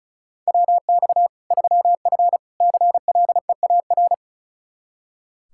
Morse code, heard as a keyed tone, about 34 wpm, 700 Hz. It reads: WX 3F CLEAR
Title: K1USN SST QSO Practice 35wpm